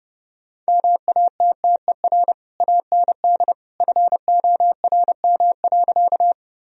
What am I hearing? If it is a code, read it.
Morse code, heard as a keyed tone, about 30 wpm, 700 Hz. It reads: MATTER AND FORM.
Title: Sentences from Top 1000 Words - Review Part 1 - 30wpm